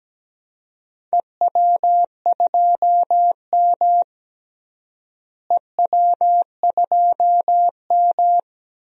Morse code, heard as a keyed tone, about 17 wpm, 700 Hz. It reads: EW2M EW2M
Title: DXCC-100 Call Signs - 4 Characters - Repeated 2x - 17wpm